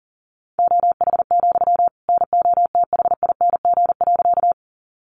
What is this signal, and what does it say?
Morse code, heard as a keyed tone, about 37 wpm, 700 Hz. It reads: OH, NOTHING.
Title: Sentences from Top 500 Words 40wpm